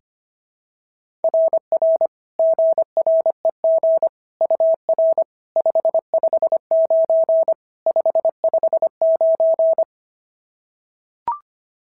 Morse code, heard as a keyed tone, about 22 wpm, 650 Hz. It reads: RR GREG UR 559 559
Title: SOTA QSO Practice 25wpm